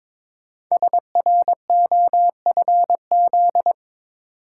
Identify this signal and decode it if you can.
Morse code, heard as a keyed tone, about 22 wpm, 700 Hz. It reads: SROFZ
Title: Code Groups - 5 Letters - 22wpm